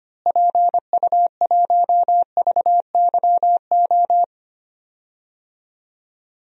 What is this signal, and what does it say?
Morse code, heard as a keyed tone, about 25 wpm, 700 Hz. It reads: PU1VYO
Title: DXCC-100 Call Signs - 6 Characters - Repeated 2x - 25wpm